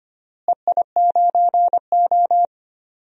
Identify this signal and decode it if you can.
Morse code, heard as a keyed tone, about 25 wpm, 700 Hz. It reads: EI9O
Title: DXCC-100 Call Signs - 4 Characters - Repeated 2x - 25wpm